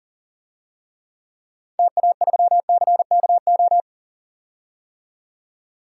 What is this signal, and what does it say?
Morse code, heard as a keyed tone, about 40 wpm, 700 Hz. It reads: TA3CKO
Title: DXCC-100 Call Signs - 6 Characters - 40wpm